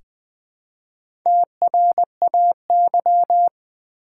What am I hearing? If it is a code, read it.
Morse code, heard as a keyed tone, about 20 wpm, 700 Hz. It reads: TRAY
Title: Easy Words with Hints - 20wpm